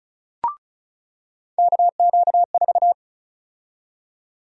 Morse code, heard as a keyed tone, about 34 wpm, 700 Hz. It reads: KQ4
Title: USA Call Sign Prefixes - 35wpm